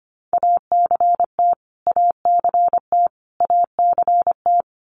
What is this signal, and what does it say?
Morse code, heard as a keyed tone, about 25 wpm, 700 Hz. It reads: ACT ACT ACT
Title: Top 500 Words Repeated 25wpm